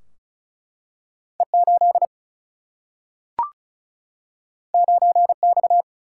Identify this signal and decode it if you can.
Morse code, heard as a keyed tone, about 36 wpm, 700 Hz. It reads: E8 9X
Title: Single-Letter Single-Number 35wpm